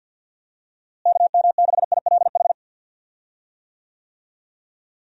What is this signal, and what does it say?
Morse code, heard as a keyed tone, about 52 wpm, 700 Hz. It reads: KM6ILH
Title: USA Call Signs - 6 Characters - Repeated 2x - 50wpm